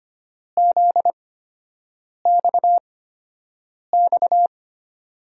Morse code, heard as a keyed tone, about 25 wpm, spaced as for 7 wpm, 700 Hz. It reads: ZXX
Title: USA Call Signs - 8x Character Spacing - 25wpm